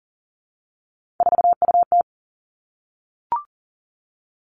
Morse code, heard as a keyed tone, about 38 wpm, 700 Hz. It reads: 4UT
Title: Three Challenging Letter-Number Combinations 40wpm